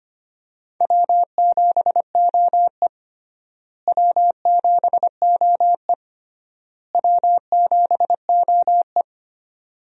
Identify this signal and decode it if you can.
Morse code, heard as a keyed tone, about 25 wpm, 700 Hz. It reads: W7OE W7OE W7OE
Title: USA Call Signs - 4 Characters - Repeated 3x - 25wpm